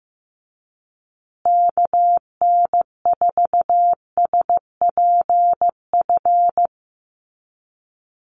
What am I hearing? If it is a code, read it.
Morse code, heard as a keyed tone, about 15 wpm, 700 Hz. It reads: KN4SPF
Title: USA Call Signs Repeated 3x 15wpm